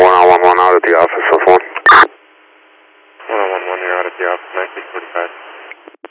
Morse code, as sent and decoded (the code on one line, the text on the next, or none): none